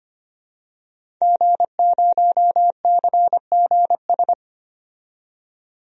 Morse code, heard as a keyed tone, about 25 wpm, 700 Hz. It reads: G0CGS
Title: DXCC-100 Call Signs - 5 Characters - 25wpm